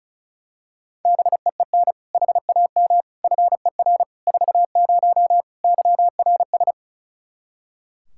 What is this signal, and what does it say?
Morse code, heard as a keyed tone, about 36 wpm, 700 Hz. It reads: BEEN HAM FER 40 YRS
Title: Practice QSO - Exchange 3 - More Info - 35wpm